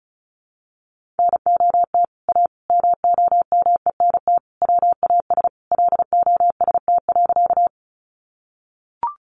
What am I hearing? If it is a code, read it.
Morse code, heard as a keyed tone, about 34 wpm, 700 Hz. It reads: NOT A MOMENT WAS LOST.
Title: Sentences from Top 1400 Words 35wpm